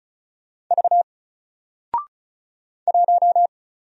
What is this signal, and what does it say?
Morse code, heard as a keyed tone, about 36 wpm, 700 Hz. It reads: V 1
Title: Single Letter-Number Alt 35wpm